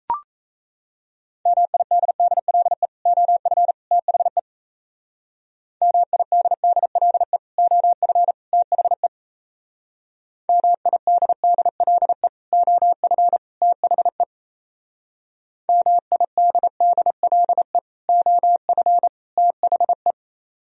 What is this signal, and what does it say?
Morse code, heard as a keyed tone, about 33 wpm, 700 Hz. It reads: MIDDLE OF THE MIDDLE OF THE MIDDLE OF THE MIDDLE OF THE
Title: Speed-Racing - Sets of 3 Words 28wpm